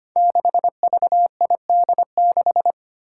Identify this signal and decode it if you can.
Morse code, heard as a keyed tone, about 25 wpm, 700 Hz. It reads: 6VID6
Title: Five Challenging Letter-Number Combinations 25wpm